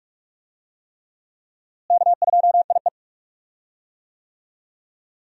Morse code, heard as a keyed tone, about 46 wpm, 700 Hz. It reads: K2IE
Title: USA Call Signs - 4 Characters - Repeated 3x - 45wpm